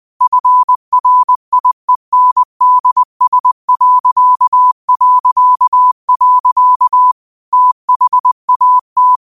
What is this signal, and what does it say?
Morse code, heard as a keyed tone, about 20 wpm, 1000 Hz. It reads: FRIENDS... THAT